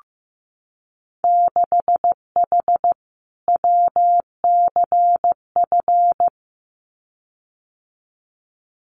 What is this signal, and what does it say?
Morse code, heard as a keyed tone, about 15 wpm, 700 Hz. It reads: 6H WCF
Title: Winter Field Day Exchanges Encore 15wpm